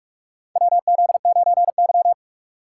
Morse code, heard as a keyed tone, about 46 wpm, 700 Hz. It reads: WZ9Y
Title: USA Call Signs - 4 Characters - 6x Character Spacing - 45wpm